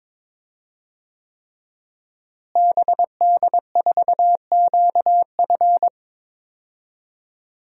Morse code, heard as a keyed tone, about 22 wpm, 700 Hz. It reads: BD4QF
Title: DXCC-100 Call Signs - 5 Characters - Repeated 3x - 22wpm